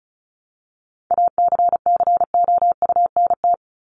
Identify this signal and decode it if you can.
Morse code, heard as a keyed tone, about 34 wpm, 700 Hz. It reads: ACCOUNT
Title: Top 751-1000 Words 35wpm